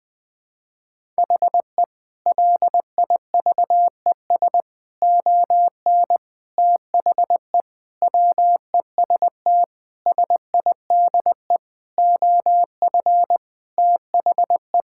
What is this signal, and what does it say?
Morse code, heard as a keyed tone, about 20 wpm, 700 Hz. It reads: HE LIVES ON THE WEST SIDE OF THE
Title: Sentences with 6-Word Phrases - 20wpm